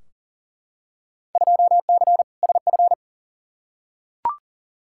Morse code, heard as a keyed tone, about 40 wpm, 700 Hz. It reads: 2C SF
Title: Field Day Exchanges Encore - 40wpm